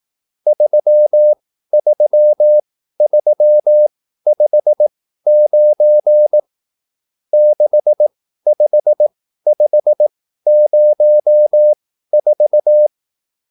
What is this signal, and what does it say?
Morse code, heard as a keyed tone, about 18 wpm, spaced as for 13 wpm, 600 Hz. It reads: 33359 65504